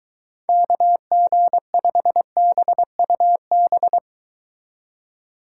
Morse code, heard as a keyed tone, about 23 wpm, 700 Hz. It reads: KG5BUB